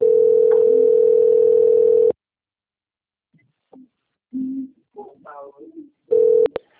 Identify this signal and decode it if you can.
Ringback tone (North America), 440 + 480 Hz.